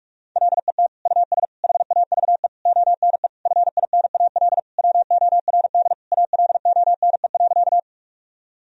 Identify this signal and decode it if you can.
Morse code, heard as a keyed tone, about 46 wpm, 700 Hz. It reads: LET US HAVE ONE FINAL WORD ALONE.
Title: Sentences from Top 1000 Words - Review Part 2 - 45wpm